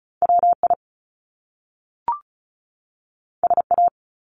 Morse code, heard as a keyed tone, about 36 wpm, 700 Hz. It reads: WI SA
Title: Top Two Letter Combinations in Words 35wpm